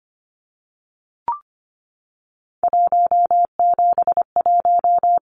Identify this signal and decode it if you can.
Morse code, heard as a keyed tone, about 25 wpm, 700 Hz. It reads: 171
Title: Numbers - 3-Digit 25wpm